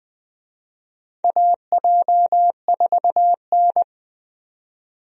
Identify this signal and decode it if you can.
Morse code, heard as a keyed tone, about 20 wpm, 700 Hz. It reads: AJ4N
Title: USA Call Signs - 4 Characters - 20wpm